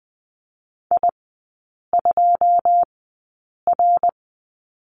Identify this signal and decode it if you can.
Morse code, heard as a keyed tone, about 20 wpm, spaced as for 8 wpm, 690 Hz. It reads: I2R
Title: 081 Lesson 14e - Number 2 - Three Characters 20wpm